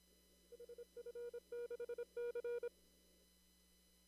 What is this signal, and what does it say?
Morse code, heard as a keyed tone, about 26 wpm, 480 Hz. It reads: HF6C
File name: CW / kt5j wpx cw-op-k5pi